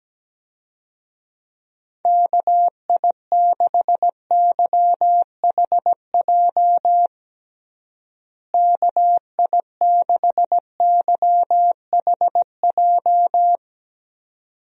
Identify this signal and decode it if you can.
Morse code, heard as a keyed tone, about 17 wpm, 700 Hz. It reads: KI6YHJ KI6YHJ
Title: USA Call Signs - 6 Characters - Repeated 2x - 17wpm